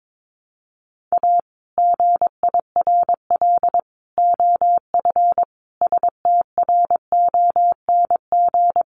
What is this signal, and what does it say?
Morse code, heard as a keyed tone, about 22 wpm, 700 Hz. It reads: A GIRL OF STRONG
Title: Sentences from Top 1200 Words 22wpm